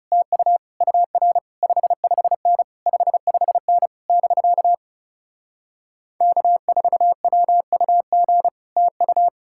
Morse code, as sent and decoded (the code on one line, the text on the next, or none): - ..- / ..- .-. / ..... ..... -. / ..... ..... -. / -...-.- / -.- ....- .-- ..- --. / - ..-
TU UR 55N 55N <BK> K4WUG TU